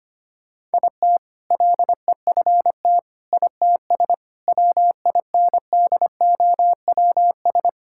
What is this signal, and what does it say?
Morse code, heard as a keyed tone, about 25 wpm, 700 Hz. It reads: IT LEFT ITS WINDOWS
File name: practice / 25wpm / All-Age Appropriate Jokes - 25wpm